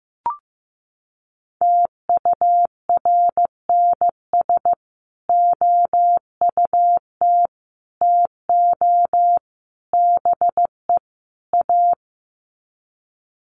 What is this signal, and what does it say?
Morse code, heard as a keyed tone, about 15 wpm, 700 Hz. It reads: TURNS OUT TO BE A
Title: Sets of 5 Words Encore 15wpm